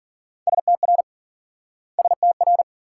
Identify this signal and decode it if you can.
Morse code, heard as a keyed tone, about 47 wpm, 700 Hz. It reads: STR STR
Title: Speed-Racing Multi-letter Phonemes - 40wpm